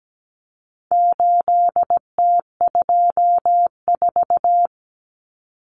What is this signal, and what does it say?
Morse code, heard as a keyed tone, about 17 wpm, 700 Hz. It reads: 8T24
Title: Common Cut Numbers - 4-Digit 17wpm